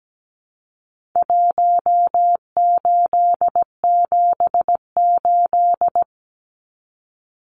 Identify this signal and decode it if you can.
Morse code, heard as a keyed tone, about 17 wpm, 700 Hz. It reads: 1878
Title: Numbers - Contest Style - 4-Digit 17wpm